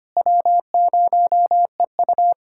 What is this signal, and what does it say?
Morse code, heard as a keyed tone, about 25 wpm, 700 Hz. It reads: W0EU
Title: USA Callsign DE Callsign 25wpm